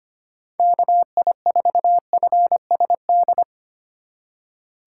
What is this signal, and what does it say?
Morse code, heard as a keyed tone, about 25 wpm, 700 Hz. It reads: KI4FSD
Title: USA Call Signs - 6 Characters - 2x Character Spacing - 25wpm